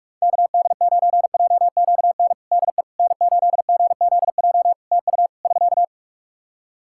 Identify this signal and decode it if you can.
Morse code, heard as a keyed tone, about 46 wpm, 700 Hz. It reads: KD9JQN DE N8GZJ TU <SK>
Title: K1USN SST QSO Practice 45wpm